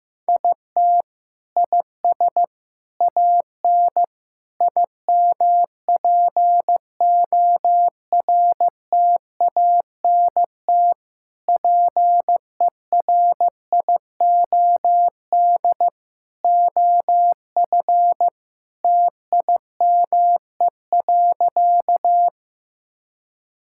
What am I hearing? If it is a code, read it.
Morse code, heard as a keyed tone, about 15 wpm, 700 Hz. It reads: IT IS AN IMPORTANT PERIOD OF TIME.
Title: Sentences from Top 1000 Words - Review Part 5 - 15wpm